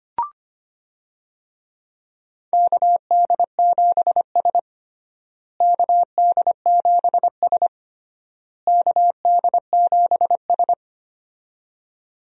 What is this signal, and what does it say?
Morse code, heard as a keyed tone, about 25 wpm, 700 Hz. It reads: KD7S KD7S KD7S
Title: USA Call Signs - 4 Characters - Repeated 3x - 25wpm